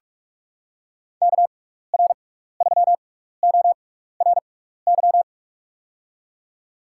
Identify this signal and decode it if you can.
Morse code, heard as a keyed tone, about 47 wpm, spaced as for 16 wpm, 700 Hz. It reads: KR3ORY
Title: USA Call Signs - 6x Character Spacing - 45wpm